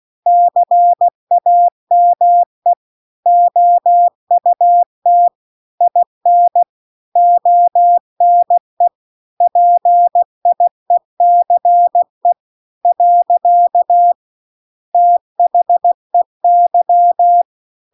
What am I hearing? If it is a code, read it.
Morse code, heard as a keyed tone, about 16 wpm, 700 Hz. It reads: CAME OUT IN ONE PIECE . THEY